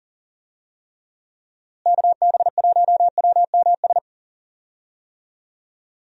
Morse code, heard as a keyed tone, about 37 wpm, 700 Hz. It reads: KB1WMS
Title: USA Call Signs - 6 Characters - Repeated 2x - 40wpm